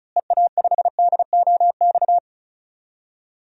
Morse code, heard as a keyed tone, about 36 wpm, 700 Hz. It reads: EA5DOX
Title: DXCC-100 Call Signs - 6 Characters - Repeated 2x - 35wpm